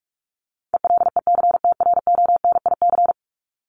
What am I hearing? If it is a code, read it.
Morse code, heard as a keyed tone, about 44 wpm, 700 Hz. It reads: ELECTRONIC
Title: Top 1001-1250 Words 45wpm